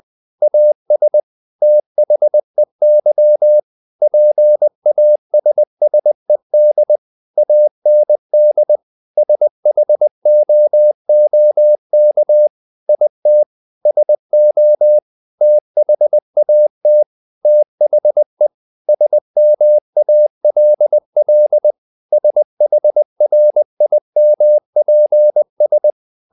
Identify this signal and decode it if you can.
Morse code, heard as a keyed tone, about 20 wpm, 600 Hz. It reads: AS THEY PASSED AND SHOOK IT SO THAT THE SMALL SHRIMPS
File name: the old man and the sea part 2 20wpm